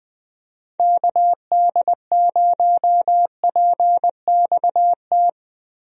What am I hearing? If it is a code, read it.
Morse code, heard as a keyed tone, about 20 wpm, 700 Hz. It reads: KD0PXT
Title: USA Call Signs - 6 Characters - Repeated 2x - 20wpm